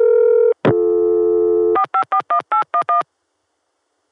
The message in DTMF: 4541811